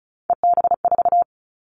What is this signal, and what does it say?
Morse code, heard as a keyed tone, about 36 wpm, 690 Hz. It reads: EB4
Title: Callsign Prefixes for Top 25 DXCC - 35wpm